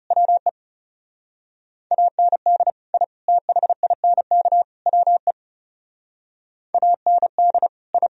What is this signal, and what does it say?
Morse code, heard as a keyed tone, about 37 wpm, 700 Hz. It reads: WE AND I THINK WE AND I
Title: Speed-Racing - Sets of 4 Words 30wpm